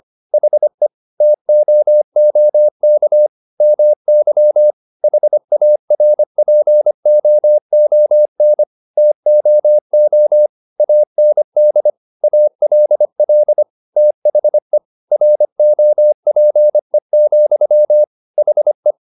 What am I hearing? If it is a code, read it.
Morse code, heard as a keyed tone, about 25 wpm, 600 Hz. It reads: HE TOOK MY HARPOON TOO AND ALL THE ROPE, HE